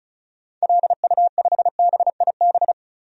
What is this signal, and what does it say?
Morse code, heard as a keyed tone, about 34 wpm, 700 Hz. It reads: LU5BIB
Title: DXCC-100 Call Signs - 6 Characters - 35wpm